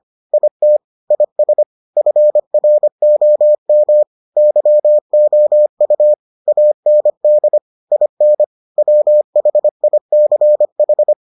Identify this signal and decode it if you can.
Morse code, heard as a keyed tone, about 25 wpm, 600 Hz. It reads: IT IS FROM YOU AND IN WHICH